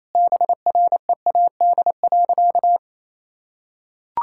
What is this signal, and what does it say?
Morse code, heard as a keyed tone, about 28 wpm, 700 Hz. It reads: BREAD.
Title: Sentences from Top 1600 Words 28wpm